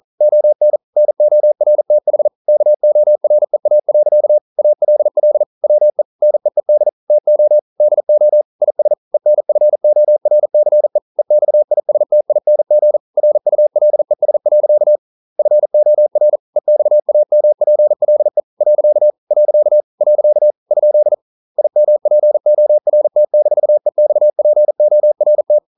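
Morse code, heard as a keyed tone, about 40 wpm, 600 Hz. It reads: ON NORTH KOREA. ALL WE NEED TO DO IS ENFORCE EXISTING RULES. FOR EXAMPLE . . ? IMPORT-EXPORT
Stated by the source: Morse code practice at 41 wpm, from CW med Gnister.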